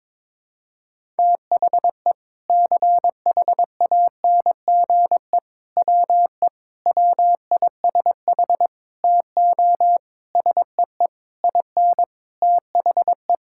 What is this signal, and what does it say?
Morse code, heard as a keyed tone, about 22 wpm, 700 Hz. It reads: THE CHANGE WE WISH TO SEE IN THE
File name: Sets of 9 Words - 22wpm